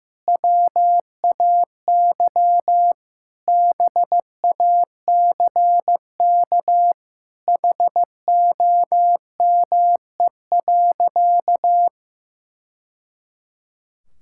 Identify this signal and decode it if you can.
Morse code, heard as a keyed tone, about 15 wpm, 700 Hz. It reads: WAY BACK HOME.
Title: Sentences from Top 1000 Words - Review Part 6 - 15wpm